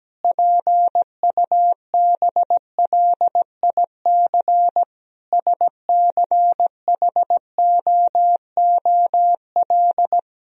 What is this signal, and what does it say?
Morse code, heard as a keyed tone, about 17 wpm, 700 Hz. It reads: PUBLIC SCHOOL
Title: Sentences from Top 1000 Words - Review Part 3 - 17wpm